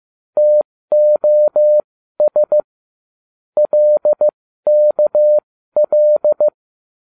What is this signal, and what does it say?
Morse code, heard as a keyed tone, about 15 wpm, spaced as for 13 wpm, 600 Hz. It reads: TOS LKL